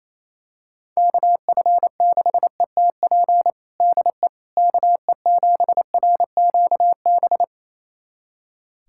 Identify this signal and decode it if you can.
Morse code, heard as a keyed tone, about 28 wpm, 700 Hz. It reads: KF6ETP DE KE7RQB
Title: USA Callsign DE Callsign - 6 Characters - 28wpm